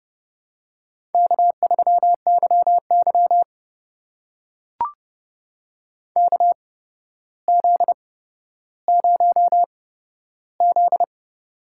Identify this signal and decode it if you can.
Morse code, heard as a keyed tone, about 30 wpm, spaced as for 13 wpm, 700 Hz. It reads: K3YY KZ0Z
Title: USA Call Signs - 4 Characters - 8x Character Spacing - 30wpm